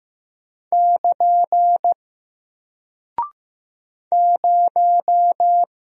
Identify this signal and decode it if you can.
Morse code, heard as a keyed tone, about 15 wpm, 700 Hz. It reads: <KN> 0
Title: Single Character (Letter:Number:Punctuation:Prosign) Alt 15wpm